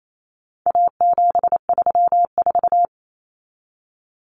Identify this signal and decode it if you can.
Morse code, heard as a keyed tone, about 28 wpm, 700 Hz. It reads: A734
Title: Common Cut Numbers - 4-Digit 28wpm